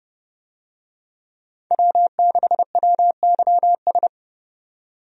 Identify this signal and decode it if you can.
Morse code, heard as a keyed tone, about 30 wpm, 700 Hz. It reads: W6WYS
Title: USA Call Signs - 5 Characters - 30wpm